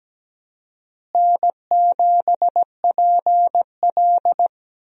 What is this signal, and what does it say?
Morse code, heard as a keyed tone, about 17 wpm, 700 Hz. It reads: N7PL
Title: USA Call Signs - 4 Characters - Repeated 3x - 17wpm